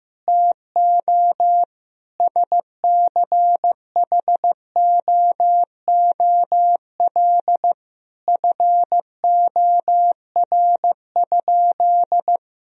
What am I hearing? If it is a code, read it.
Morse code, heard as a keyed tone, about 16 wpm, 700 Hz. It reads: TO SCHOOL FOR?
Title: Sentences from Top 1000 Words - Review Part 1 - 15wpm